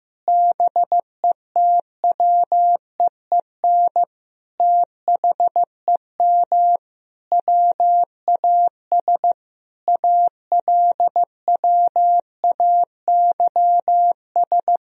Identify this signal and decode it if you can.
Morse code, heard as a keyed tone, about 15 wpm, 700 Hz. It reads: BETWEEN THEM WAS ALWAYS
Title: Sentences from Top 1000 Words - Review Part 1 - 15wpm